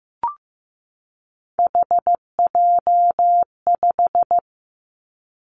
Morse code, heard as a keyed tone, about 15 wpm, 700 Hz. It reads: HJ5
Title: Callsign Prefixes for Top 100 DXCC - 15wpm